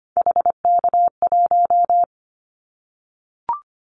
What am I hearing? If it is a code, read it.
Morse code, heard as a keyed tone, about 25 wpm, 700 Hz. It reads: HK1
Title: Callsign Prefixes for Top 100 DXCC - 25wpm